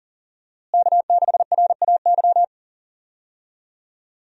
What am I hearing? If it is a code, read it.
Morse code, heard as a keyed tone, about 40 wpm, 700 Hz. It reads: K6RAY
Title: USA Call Signs - 5 Characters - Repeated 2x - 40wpm